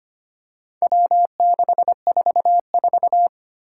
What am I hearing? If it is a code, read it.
Morse code, heard as a keyed tone, about 25 wpm, 700 Hz. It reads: W644
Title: Four Challenging Letter-Number Combinations 25wpm